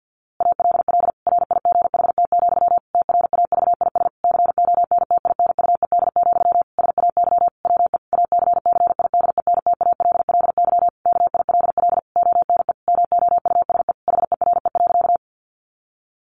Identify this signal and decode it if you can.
Morse code, heard as a keyed tone, about 50 wpm, 700 Hz. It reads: ALL RIGHT, TRAVIS CONTINUED, SAY WE ACCIDENTALLY KILL ONE MOUSE HERE.
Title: A Sound Of Thunder 50wpm